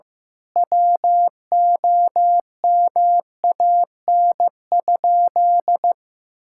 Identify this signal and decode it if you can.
Morse code, heard as a keyed tone, about 15 wpm, 700 Hz. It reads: WOMAN?
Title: Sentences from Top 1600 Words 15wpm